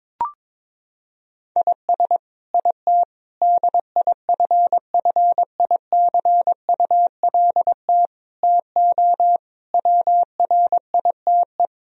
Morse code, heard as a keyed tone, about 22 wpm, 700 Hz. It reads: IS IT DIFFICULT TO WRITE
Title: Sentences from Top 1000 Words - Review Part 3 - 22wpm